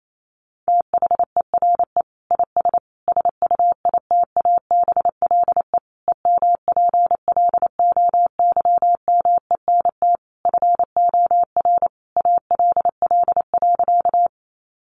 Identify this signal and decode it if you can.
Morse code, heard as a keyed tone, about 28 wpm, 700 Hz. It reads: THERE IS SUITABLE EMPLOYMENT FOR ALL.
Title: Sentences from Top 1500 Words 28wpm